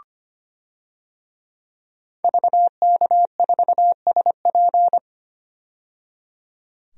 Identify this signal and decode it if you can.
Morse code, heard as a keyed tone, about 25 wpm, 700 Hz. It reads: VK4SP
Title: DXCC-100 Call Signs - 5 Characters - 25wpm